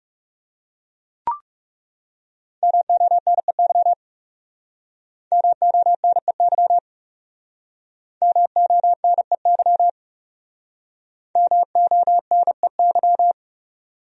Morse code, heard as a keyed tone, about 40 wpm, 700 Hz. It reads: MONEY MONEY MONEY MONEY